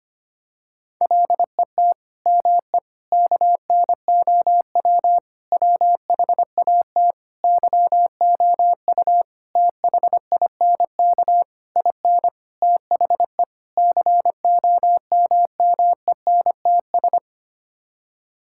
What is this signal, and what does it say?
Morse code, heard as a keyed tone, about 25 wpm, 700 Hz. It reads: LET ME KNOW WHAT YOU THINK IN THE COMMENTS
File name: Sets of 9 Words - 25wpm